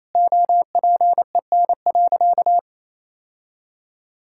Morse code, heard as a keyed tone, about 28 wpm, 700 Hz. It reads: OPEN.
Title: Sentences from Top 1600 Words 28wpm